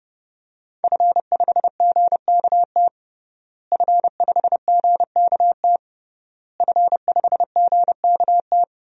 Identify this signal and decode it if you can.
Morse code, heard as a keyed tone, about 32 wpm, 700 Hz. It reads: F5GKT F5GKT F5GKT
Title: DXCC-100 Call Signs - 5 Characters - Repeated 3x - 30wpm